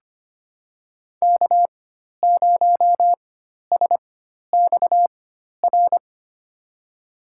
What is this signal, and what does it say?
Morse code, heard as a keyed tone, about 25 wpm, spaced as for 12 wpm, 700 Hz. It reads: K0SXR
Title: USA Call Signs - 5 Characters - 4x Character Spacing - 25wpm